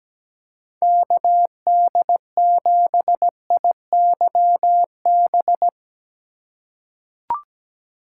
Morse code, heard as a keyed tone, about 17 wpm, 700 Hz. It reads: KD7IYB
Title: USA Call Signs - 6 Characters - 3x Character Spacing - 17wpm